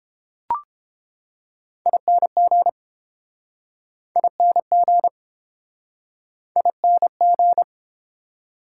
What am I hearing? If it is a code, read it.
Morse code, heard as a keyed tone, about 30 wpm, 700 Hz. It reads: ING ING ING